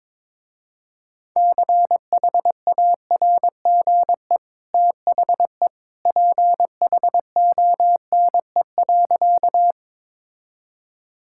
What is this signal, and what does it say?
Morse code, heard as a keyed tone, about 22 wpm, 690 Hz. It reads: CHARGE THE PHONE.
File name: practice / 22wpm / Easy Sentences with Hints - 22wpm